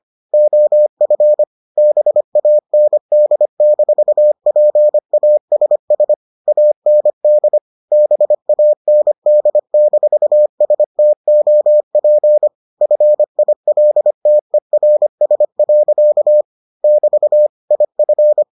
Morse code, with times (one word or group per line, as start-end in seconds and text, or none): 0.3-1.4 OF
1.8-6.1 BAND-PASS
6.5-7.6 AND
7.9-12.5 BAND-STOP
12.8-16.4 FILTERS.
16.8-17.5 <BT>
17.7-18.4 IF